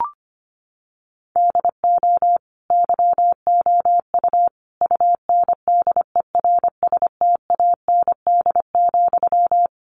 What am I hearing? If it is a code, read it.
Morse code, heard as a keyed tone, about 25 wpm, 700 Hz. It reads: DO YOU UNDERSTAND,
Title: Sentences from Top 1200 Words 25wpm